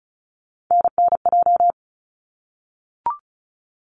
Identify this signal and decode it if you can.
Morse code, heard as a keyed tone, about 34 wpm, 700 Hz. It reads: NNJ
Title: ARRL Sections 35wpm